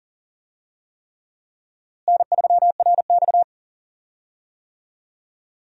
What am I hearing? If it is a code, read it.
Morse code, heard as a keyed tone, about 40 wpm, 700 Hz. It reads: N3RX
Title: USA Call Signs - 4 Characters - Repeated 3x - 40wpm